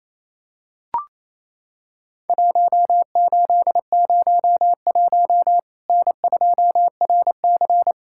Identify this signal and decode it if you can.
Morse code, heard as a keyed tone, about 28 wpm, 700 Hz. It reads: 1801 N2RC
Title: Worked All Europe DX Contest - QTC Exchange - 28wpm